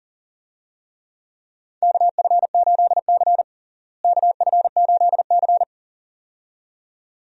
Morse code, heard as a keyed tone, about 40 wpm, 700 Hz. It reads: KF8C KF8C
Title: USA Call Signs - 4 Characters - Repeated 2x - 40wpm